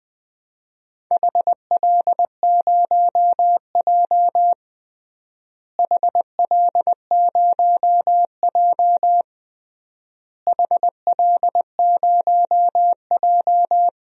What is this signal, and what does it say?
Morse code, heard as a keyed tone, about 21 wpm, 700 Hz. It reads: HL0J HL0J HL0J
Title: DXCC-100 Call Signs - 4 Characters - Repeated 3x - 20wpm